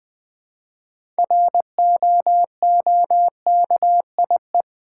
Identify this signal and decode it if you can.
Morse code, heard as a keyed tone, about 20 wpm, 700 Hz. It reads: ROOKIE
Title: Build A Word - 2 Letters At A Time - 6-Letter Words - 20wpm